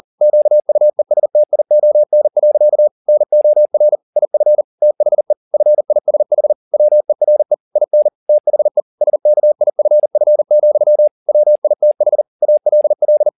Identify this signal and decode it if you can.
Morse code, heard as a keyed tone, about 40 wpm, 600 Hz. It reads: QUESTION. NOR IF THE FISH WERE IN THE SKIFF, WITH ALL
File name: the old man and the sea 2 40wpm